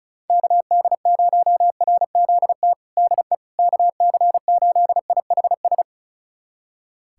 Morse code, heard as a keyed tone, about 36 wpm, 700 Hz. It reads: KD0RZT DE KC8IHS
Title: USA Callsign DE Callsign 35wpm